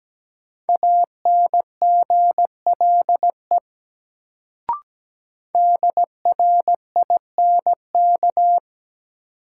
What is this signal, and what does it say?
Morse code, heard as a keyed tone, about 17 wpm, 700 Hz. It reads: ANGLE DRINK
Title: Sentences from Top 1500 Words - New Word Drill - 17wpm